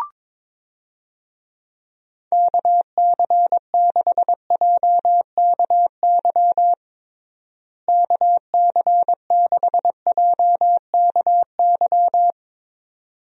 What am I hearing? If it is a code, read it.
Morse code, heard as a keyed tone, about 22 wpm, 700 Hz. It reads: KC6JKY KC6JKY
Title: USA Call Signs - 6 Characters - Repeated 2x - 22wpm